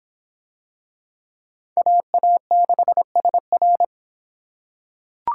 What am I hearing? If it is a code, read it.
Morse code, heard as a keyed tone, about 26 wpm, 700 Hz. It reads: AA6SR